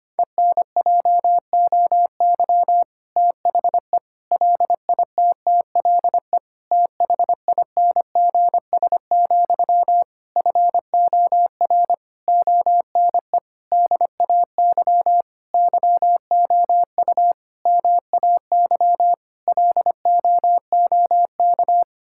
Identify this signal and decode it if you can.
Morse code, heard as a keyed tone, about 25 wpm, 700 Hz. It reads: ENJOY THE LITTLE THINGS, FOR ONE DAY YOU MAY LOOK
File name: practice / 25wpm / Difficult Sentences with Hints - 25wpm